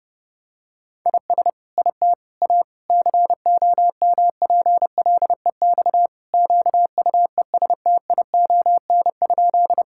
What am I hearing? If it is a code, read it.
Morse code, heard as a keyed tone, about 32 wpm, 700 Hz. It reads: IS IT A COMPLEX QUESTION?
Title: Sentences from Top 1000 Words - Review Part 6 - 30wpm